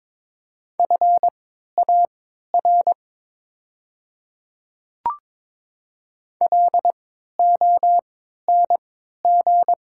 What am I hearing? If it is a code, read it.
Morse code, heard as a keyed tone, about 22 wpm, spaced as for 12 wpm, 700 Hz. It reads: FAR LONG
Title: 3x Character Spacing - Top 300 Words - 22wpm